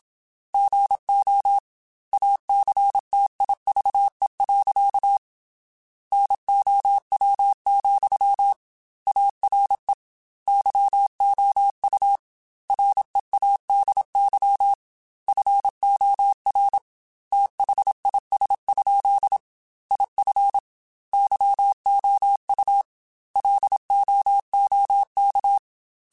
Morse code, heard as a keyed tone, about 27 wpm, 800 Hz. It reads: GO ACTIVE. NOW, ARE YOU READY FOR THIS? IF YOU LOOK